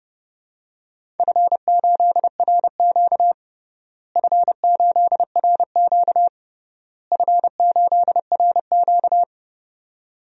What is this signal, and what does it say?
Morse code, heard as a keyed tone, about 30 wpm, 700 Hz. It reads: F8RQ F8RQ F8RQ
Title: DXCC-100 Call Signs - 4 Characters - Repeated 3x - 30wpm